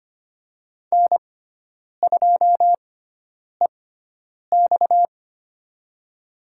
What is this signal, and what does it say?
Morse code, heard as a keyed tone, about 25 wpm, spaced as for 9 wpm, 700 Hz. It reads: N2EX
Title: USA Call Signs - 6x Character Spacing - 25wpm